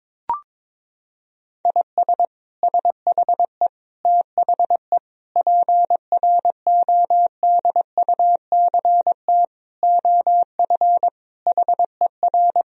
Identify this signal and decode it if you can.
Morse code, heard as a keyed tone, about 22 wpm, 700 Hz. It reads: IS SHE THE PRODUCT OF HER
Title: Sentences from Top 1000 Words - Review Part 2 - 22wpm